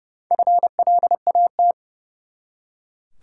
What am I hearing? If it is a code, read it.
Morse code, heard as a keyed tone, about 30 wpm, 700 Hz. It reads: FLAT
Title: Four Letter Words - 30wpm